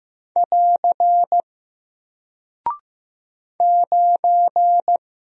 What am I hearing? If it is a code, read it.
Morse code, heard as a keyed tone, about 15 wpm, 700 Hz. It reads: <AR> 9
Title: Single Character (Letter:Number:Punctuation:Prosign) 15wpm